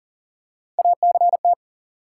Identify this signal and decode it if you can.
Morse code, heard as a keyed tone, about 40 wpm, 700 Hz. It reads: ACT
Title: Three Letter Words - 40wpm